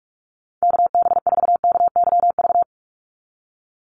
Morse code, heard as a keyed tone, about 44 wpm, 700 Hz. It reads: KB4KYV
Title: USA Call Signs 45wpm